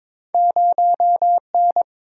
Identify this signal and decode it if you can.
Morse code, heard as a keyed tone, about 30 wpm, 700 Hz. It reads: 0N